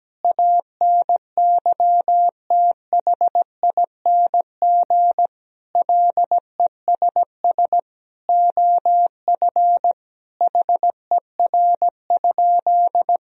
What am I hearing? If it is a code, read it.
Morse code, heard as a keyed tone, about 17 wpm, 700 Hz. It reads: ANYTHING LESS OF HER?
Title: Sentences from Top 1000 Words - Review Part 2 - 17wpm